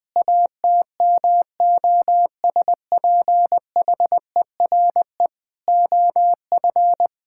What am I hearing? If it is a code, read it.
Morse code, heard as a keyed tone, about 20 wpm, 700 Hz. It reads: ATMOSPHERE OF
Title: Sentences from Top 1600 Words 20wpm